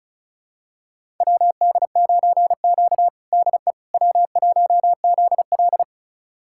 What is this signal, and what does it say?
Morse code, heard as a keyed tone, about 36 wpm, 700 Hz. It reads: WD9Q DE W1ZL
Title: USA Callsign DE Callsign - 4 Characters - 35wpm